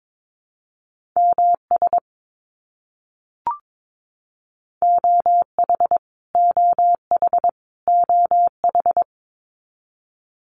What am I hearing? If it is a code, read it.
Morse code, heard as a keyed tone, about 22 wpm, 700 Hz. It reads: MS OH OH OH
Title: US States Repeated 3x 22wpm